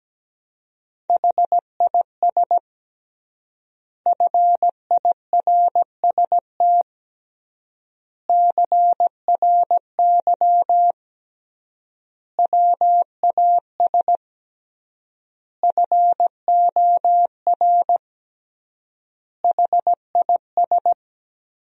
Morse code, heard as a keyed tone, about 17 wpm, 700 Hz. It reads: HIS FIRST CRY WAS FOR HIS
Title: 3x Word-Spacing - Sentences from Top 500 Words - 17wpm